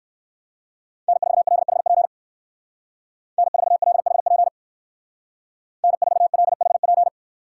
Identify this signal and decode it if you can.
Morse code, heard as a keyed tone, about 54 wpm, 700 Hz. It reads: N4LHP N4LHP N4LHP
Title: Speed Racing - USA 1x3 Call Signs 45wpm